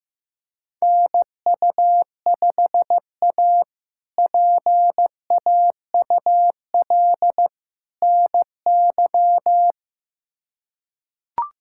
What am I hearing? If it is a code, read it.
Morse code, heard as a keyed tone, about 15 wpm, 700 Hz. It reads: NU5A PAUL NY
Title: K1USN SST QSO Practice - 15wpm